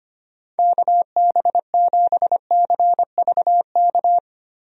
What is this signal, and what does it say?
Morse code, heard as a keyed tone, about 25 wpm, 700 Hz. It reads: KB7CVK
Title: USA Call Signs - 6 Characters - Repeated 2x - 25wpm